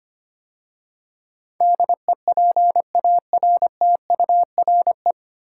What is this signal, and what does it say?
Morse code, heard as a keyed tone, about 25 wpm, 690 Hz. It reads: DEPARTURE
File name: practice / 25wpm / 9 Letter Words - First 3 Letters Given - 25wpm